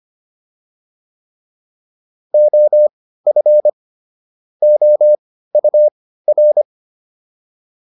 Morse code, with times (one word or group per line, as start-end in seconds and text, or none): 2.3-3.7 OF
4.6-6.6 OUR